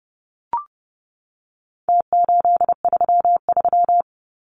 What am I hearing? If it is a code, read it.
Morse code, heard as a keyed tone, about 30 wpm, 700 Hz. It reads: T833
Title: Numbers - Contest Style - 4-Digit 30wpm